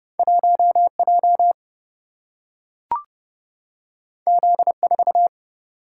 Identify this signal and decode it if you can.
Morse code, heard as a keyed tone, about 30 wpm, 700 Hz. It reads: 1J Z4
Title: Single-Letter Single-Number 30wpm